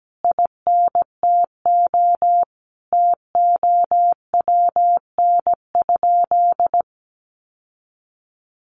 Morse code, heard as a keyed tone, about 17 wpm, 700 Hz. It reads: INTO TOWN?
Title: Sentences from Top 1400 Words 17wpm